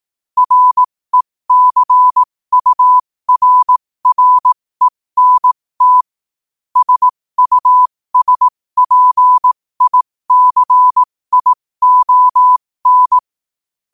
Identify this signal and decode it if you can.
Morse code, heard as a keyed tone, about 18 wpm, spaced as for 15 wpm, 1000 Hz. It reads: RECURRENT SUSPICION